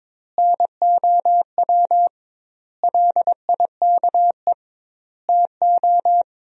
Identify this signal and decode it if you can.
Morse code, heard as a keyed tone, about 22 wpm, 700 Hz. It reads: NOW LIKE TO
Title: 2x Word-Spacing - Sentences from Top 200 Words - 22wpm